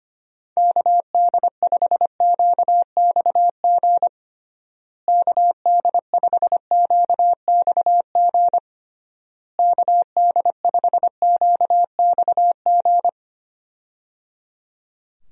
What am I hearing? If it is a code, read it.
Morse code, heard as a keyed tone, about 25 wpm, 700 Hz. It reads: KD5QXG KD5QXG KD5QXG
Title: USA Call Signs - 6 Characters - Repeated 3x - 25wpm